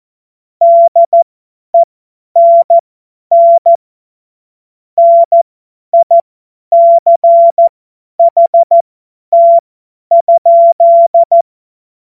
Morse code, heard as a keyed tone, about 14 wpm, spaced as for 10 wpm, 680 Hz. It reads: DENN NICHT?
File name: Morsekurs / Witze / Witz 06. April 2021, 50 F70